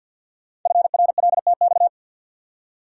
Morse code, heard as a keyed tone, about 50 wpm, 690 Hz. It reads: FRLTX
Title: Code Groups - 5 Letters - 50wpm